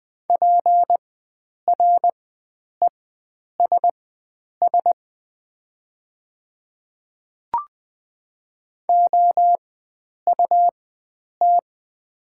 Morse code, heard as a keyed tone, about 20 wpm, spaced as for 9 wpm, 700 Hz. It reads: PRESS OUT
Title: 4x Character Spacing - Top 300 Words - 20wpm